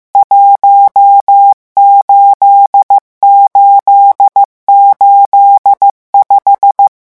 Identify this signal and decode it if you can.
Morse code, heard as a keyed tone, about 15 wpm, 800 Hz. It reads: 18885
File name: morse mp3 / revision11 chiffres vitesse 14